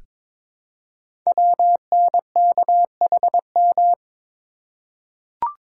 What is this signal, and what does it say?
Morse code, heard as a keyed tone, about 22 wpm, 700 Hz. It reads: WNKHM
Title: Code Groups - 5 Letters - 22wpm